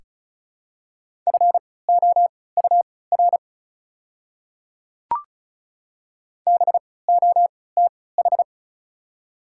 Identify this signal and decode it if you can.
Morse code, heard as a keyed tone, about 36 wpm, spaced as for 20 wpm, 700 Hz. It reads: FOUR BOTH
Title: 3x Character Spacing - Top 500 Words - 35wpm